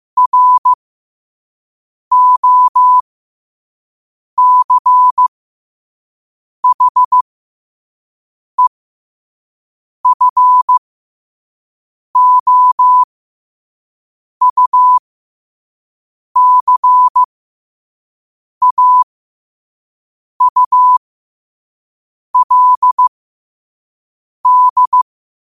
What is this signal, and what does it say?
Morse code, heard as a keyed tone, about 15 wpm, spaced as for 5 wpm, 1000 Hz. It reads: ROCHEFOUCAULD
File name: qotdfiles / 5wpm